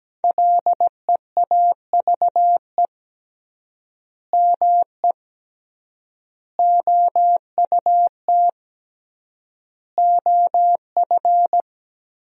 Morse code, heard as a keyed tone, about 17 wpm, 700 Hz. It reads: LEAVE ME OUT OF
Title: 3x Word-Spacing - Sentences from Top 400 Words - 17wpm